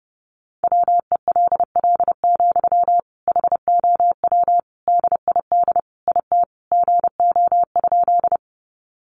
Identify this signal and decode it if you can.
Morse code, heard as a keyed tone, about 30 wpm, 700 Hz. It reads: WELL, HOW DID IT GO?
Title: Sentences from Top 200 Words 30wpm